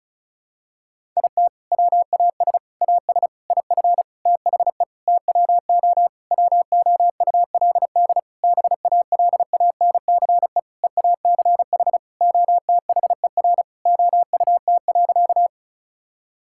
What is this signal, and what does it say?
Morse code, heard as a keyed tone, about 36 wpm, 700 Hz. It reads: IT WAS AS IF THE TWO WOULD BALANCE EACH OTHER OUT.
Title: Sentences from Top 1000 Words - Review Part 6 - 35wpm